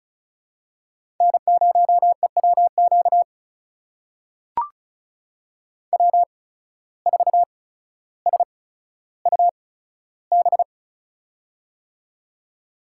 Morse code, heard as a keyed tone, about 34 wpm, spaced as for 15 wpm, 700 Hz. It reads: N0EWQ W4SUB
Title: USA Call Signs - 5 Characters - 8x Character Spacing - 35wpm